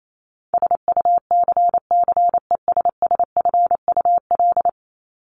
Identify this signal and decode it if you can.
Morse code, heard as a keyed tone, about 28 wpm, 700 Hz. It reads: SUCCESSFUL
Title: Top 751-1000 Words 28wpm